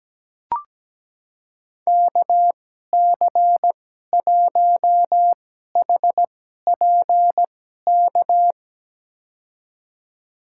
Morse code, heard as a keyed tone, about 17 wpm, spaced as for 12 wpm, 700 Hz. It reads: KC1HPK
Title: USA Call Signs - 6 Characters - 2x Character Spacing - 17wpm